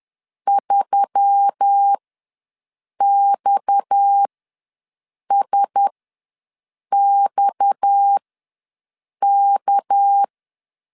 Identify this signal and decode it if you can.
Morse code, heard as a keyed tone, about 10 wpm, spaced as for 6 wpm, 800 Hz. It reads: 3XSXK
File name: LEZIONI / 6 PKX0 / 6 PKX ws 10wpm 3706